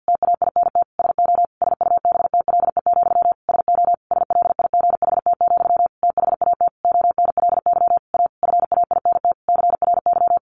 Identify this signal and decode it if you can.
Morse code, heard as a keyed tone, about 50 wpm, 700 Hz. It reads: TAINT SO SUBTLE, SO SLIGHT, THAT ONLY A FAINT CRY
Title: A Sound Of Thunder 50wpm